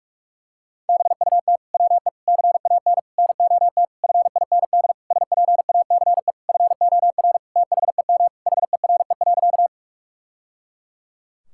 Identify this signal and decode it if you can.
Morse code, heard as a keyed tone, about 46 wpm, 700 Hz. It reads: BUT WE CAN NOT FIND SPACE FOR THEM HERE.
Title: Sentences from Top 1000 Words - Review Part 5 - 45wpm